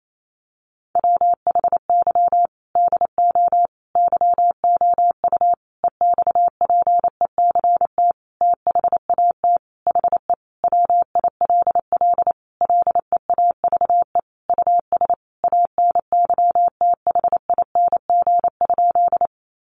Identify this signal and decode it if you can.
Morse code, heard as a keyed tone, about 28 wpm, 700 Hz. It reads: WHY DO YOU EXPECT THAT HE WILL LEAVE US ANYTHING?
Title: Sentences from Top 1100 Words 28wpm